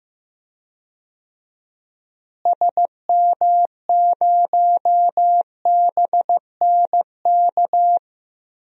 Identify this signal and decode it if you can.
Morse code, heard as a keyed tone, about 15 wpm, 700 Hz. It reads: SM0BNK
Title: DXCC-100 Call Signs - 6 Characters - Repeated 3x - 15wpm